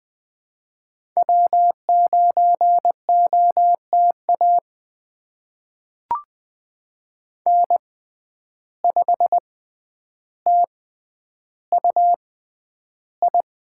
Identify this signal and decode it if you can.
Morse code, heard as a keyed tone, about 20 wpm, spaced as for 10 wpm, 700 Hz. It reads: W9OTA N5TUI
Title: USA Call Signs - 5 Characters - 6x Character Spacing - 20wpm